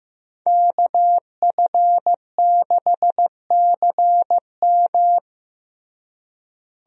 Morse code, heard as a keyed tone, about 15 wpm, 700 Hz. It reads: KF6CM
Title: USA Call Signs - 5 Characters - Repeated 2x - 15wpm